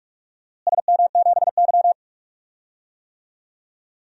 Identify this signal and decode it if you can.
Morse code, heard as a keyed tone, about 41 wpm, 700 Hz. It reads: SM7Y
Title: DXCC-100 Call Signs - 4 Characters - Repeated 2x - 45wpm